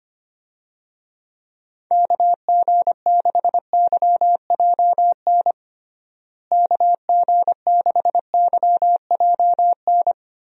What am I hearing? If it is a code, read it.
Morse code, heard as a keyed tone, about 25 wpm, 700 Hz. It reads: KG6YJN KG6YJN
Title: USA Call Signs - 6 Characters - Repeated 2x - 25wpm